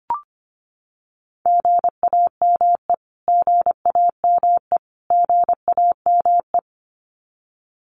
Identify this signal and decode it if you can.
Morse code, heard as a keyed tone, about 25 wpm, 700 Hz. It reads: GAME GAME GAME
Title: Top 500 Words Repeated 25wpm